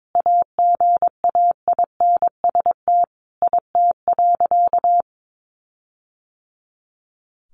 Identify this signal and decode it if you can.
Morse code, heard as a keyed tone, about 22 wpm, 700 Hz. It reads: AGAINST IT.
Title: Sentences from Top 1400 Words 22wpm